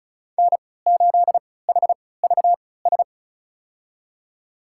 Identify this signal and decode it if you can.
Morse code, heard as a keyed tone, about 36 wpm, spaced as for 20 wpm, 700 Hz. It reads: N8HVS
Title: USA Call Signs - 5 Characters - 3x Character Spacing - 35wpm